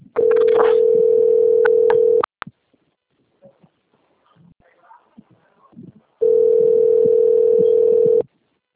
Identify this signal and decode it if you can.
Ringback tone (North America), 440 + 480 Hz.